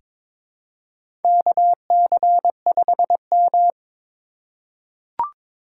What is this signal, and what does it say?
Morse code, heard as a keyed tone, about 23 wpm, 700 Hz. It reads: KC5M
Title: USA Call Signs - 4 Characters - 2x Character Spacing - 22wpm